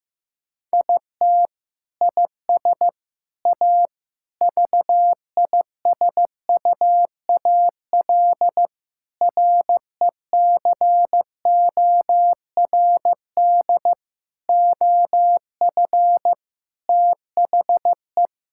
Morse code, heard as a keyed tone, about 15 wpm, 700 Hz. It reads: IT IS A VISUAL RECORD OF THE
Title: Sentences from Top 1000 Words - Review Part 6 - 15wpm